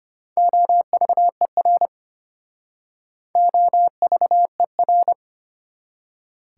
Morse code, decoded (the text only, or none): OVER OVER